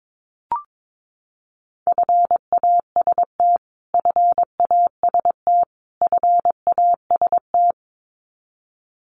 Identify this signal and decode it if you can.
Morse code, heard as a keyed tone, about 22 wpm, 700 Hz. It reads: FAST FAST FAST
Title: Top 500 Words Repeated 22wpm